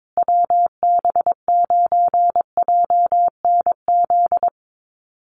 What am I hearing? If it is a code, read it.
Morse code, heard as a keyed tone, about 22 wpm, 700 Hz. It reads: WB9JNZ
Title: USA Call Signs Repeated 2x 22wpm